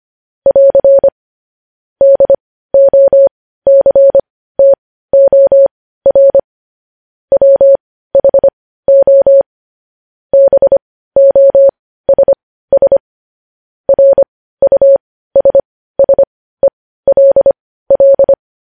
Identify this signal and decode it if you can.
Morse code, heard as a keyed tone, about 25 wpm, spaced as for 15 wpm, 550 Hz. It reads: <AR> DOCTOR WHO BOSS RUSSELL
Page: news headlines in morse code at 15 wpm